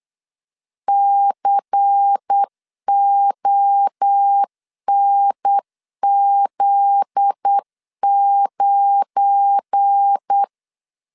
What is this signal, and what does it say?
Morse code, heard as a keyed tone, about 8 wpm, 800 Hz. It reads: CONZ9
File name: LEZIONI / 7 GF9Z / 7 GF9Z 8wpm 3213